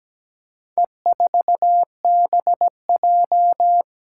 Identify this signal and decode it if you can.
Morse code, heard as a keyed tone, about 17 wpm, 700 Hz. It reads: E4BJ
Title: Four Challenging Letter-Number Combinations 17wpm